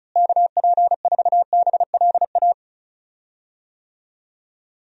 Morse code, heard as a keyed tone, about 36 wpm, 700 Hz. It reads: KP4BLA
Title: DXCC-100 Call Signs - 6 Characters - Repeated 3x - 35wpm